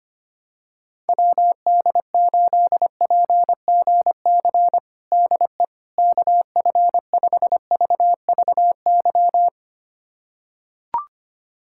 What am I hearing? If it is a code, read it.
Morse code, heard as a keyed tone, about 25 wpm, 700 Hz. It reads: WD8PGC DE KF5VVY
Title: USA Callsign DE Callsign - 6 Characters - 25wpm